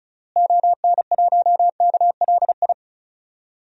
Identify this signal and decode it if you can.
Morse code, heard as a keyed tone, about 36 wpm, 700 Hz. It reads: ON1KLI
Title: DXCC-100 Call Signs - 6 Characters - Repeated 3x - 35wpm